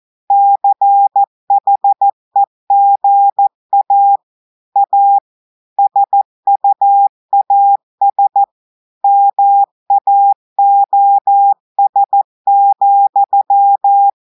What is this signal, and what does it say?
Morse code, heard as a keyed tone, about 14 wpm, 800 Hz. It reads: CHEGA A SUAS MAOS,
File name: audios / portugais / portugais texte en clair 1 vitesse 14 mots minutes